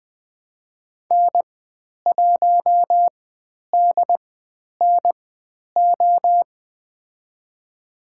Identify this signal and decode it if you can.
Morse code, heard as a keyed tone, about 20 wpm, spaced as for 10 wpm, 700 Hz. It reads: N1DNO
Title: 095 Lesson 16g - Letter c - Call Signs 20wpm